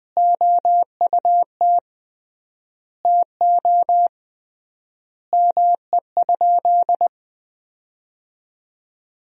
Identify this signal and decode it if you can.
Morse code, heard as a keyed tone, about 20 wpm, 700 Hz. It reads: OUT TO ME?
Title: 3x Word-Spacing - Sentences from Top 200 Words - 20wpm